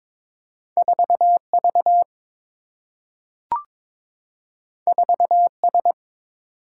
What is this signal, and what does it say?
Morse code, heard as a keyed tone, about 22 wpm, 700 Hz. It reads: 4V 4S
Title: Single-Letter Single-Number 22wpm